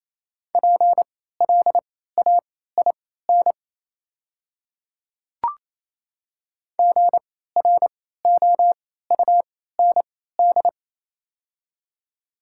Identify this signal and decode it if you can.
Morse code, heard as a keyed tone, about 28 wpm, spaced as for 16 wpm, 700 Hz. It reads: PLAIN GROUND
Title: 3x Character Spacing - Top 500 Words - 28wpm